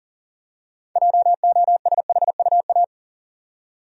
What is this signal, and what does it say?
Morse code, heard as a keyed tone, about 40 wpm, 700 Hz. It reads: JOSHUA
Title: USA Common Names - 40wpm